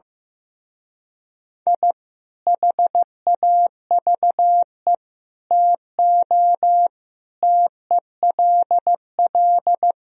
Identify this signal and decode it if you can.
Morse code, heard as a keyed tone, about 16 wpm, 700 Hz. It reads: I HAVE TO TELL
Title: Speed-Racing - Sets of 5 Words 15wpm